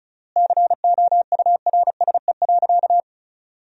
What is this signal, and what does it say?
Morse code, heard as a keyed tone, about 34 wpm, 700 Hz. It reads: COURSE.
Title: 3x Word-Spacing - Sentences from Top 500 Words - 35wpm